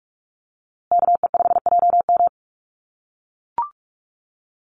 Morse code, heard as a keyed tone, about 44 wpm, 700 Hz. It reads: KE5JM
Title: USA Call Signs Repeated 2x 45wpm